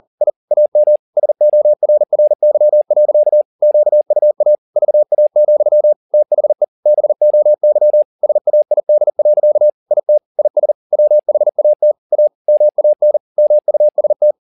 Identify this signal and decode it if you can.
Morse code, heard as a keyed tone, about 40 wpm, 600 Hz. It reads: I AM SORRY. QUA VA, THE BOY SAID. IT IS WHAT A MAN MUST